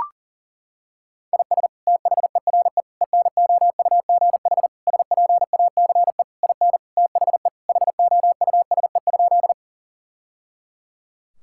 Morse code, heard as a keyed tone, about 37 wpm, 700 Hz. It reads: IS THERE ENOUGH SPACE IN THE HOUSE?
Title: Sentences from Top 1000 Words - Review Part 5 - 40wpm